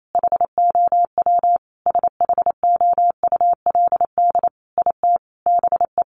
Morse code, heard as a keyed tone, about 28 wpm, 700 Hz. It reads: HOW SHOULD IT BE
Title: Sentences from Top 1200 Words 28wpm